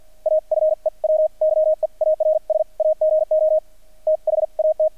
Morse code, heard as a keyed tone, about 48 wpm, spaced as for 41 wpm, 630 Hz. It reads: A FEW YEARS AGO THAT
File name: archived / CRSnet